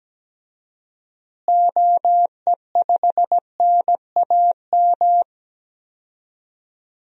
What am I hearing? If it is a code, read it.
Morse code, heard as a keyed tone, about 17 wpm, 700 Hz. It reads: OE5NAM
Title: DXCC-100 Call Signs - 6 Characters - Repeated 2x - 17wpm